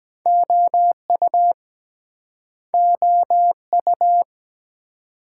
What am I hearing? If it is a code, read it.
Morse code, heard as a keyed tone, about 24 wpm, 700 Hz. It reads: OU OU